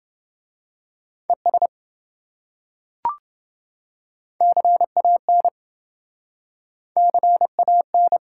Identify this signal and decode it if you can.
Morse code, heard as a keyed tone, about 27 wpm, 700 Hz. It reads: ES CAN CAN